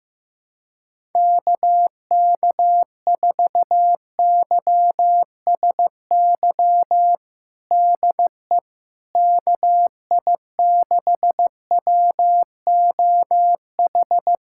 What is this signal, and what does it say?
Morse code, heard as a keyed tone, about 15 wpm, 700 Hz. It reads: KK4YSY DE KI6WOH
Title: USA Callsign DE Callsign 15wpm